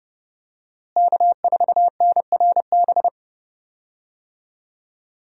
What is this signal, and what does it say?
Morse code, heard as a keyed tone, about 32 wpm, 700 Hz. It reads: K4NRB
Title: USA Call Signs - 5 Characters - Repeated 3x - 30wpm